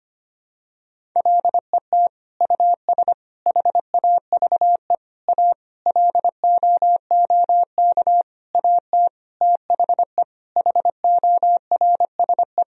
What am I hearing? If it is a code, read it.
Morse code, heard as a keyed tone, about 25 wpm, 700 Hz. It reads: LET US HAVE A LOOK AT THE HORSE
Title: Sentences from Top 1000 Words - Review Part 1 - 25wpm